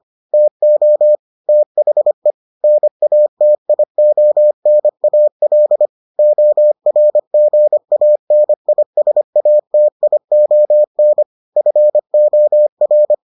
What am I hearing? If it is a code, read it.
Morse code, heard as a keyed tone, about 25 wpm, 600 Hz. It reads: TO THE NATIONAL ORGANISATION FOR